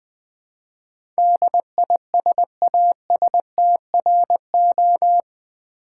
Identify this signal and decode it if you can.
Morse code, heard as a keyed tone, about 20 wpm, 700 Hz. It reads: DISASTRO
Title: Build A Word - 4 Letters At A Time - 10-Letter Words - 20wpm